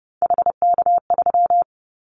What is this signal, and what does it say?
Morse code, heard as a keyed tone, about 28 wpm, 700 Hz. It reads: HK3
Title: Callsign Prefixes for Top 100 DXCC - 30wpm